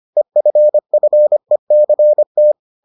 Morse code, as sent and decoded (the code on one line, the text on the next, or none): . ..-. ..-. -.-. -
EFFECT